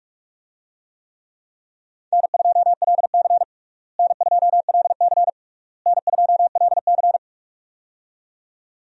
Morse code, heard as a keyed tone, about 42 wpm, 700 Hz. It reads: N2LC N2LC N2LC
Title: USA Call Signs - 4 Characters - Repeated 3x - 45wpm